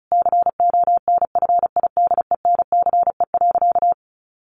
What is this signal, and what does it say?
Morse code, heard as a keyed tone, about 34 wpm, 700 Hz. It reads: CONFIDENCE.
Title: Sentences from Top 1500 Words 35wpm